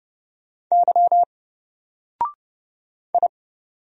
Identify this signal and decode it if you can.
Morse code, heard as a keyed tone, about 31 wpm, 700 Hz. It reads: Y I
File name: Single-Letter-Number-30wpm